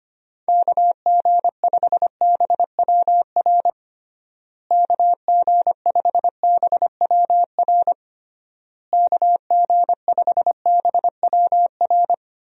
Morse code, heard as a keyed tone, about 25 wpm, 700 Hz. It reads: KG5BWR KG5BWR KG5BWR
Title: USA Call Signs - 6 Characters - Repeated 3x - 25wpm